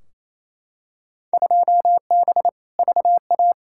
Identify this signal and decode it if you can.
Morse code, heard as a keyed tone, about 28 wpm, 700 Hz. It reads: 2B VA
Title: Field Day Exchanges Encore - 28wpm